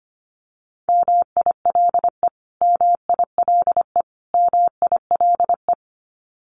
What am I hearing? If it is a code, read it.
Morse code, heard as a keyed tone, about 25 wpm, 700 Hz. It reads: MILE MILE MILE
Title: Top 500 Words Repeated 25wpm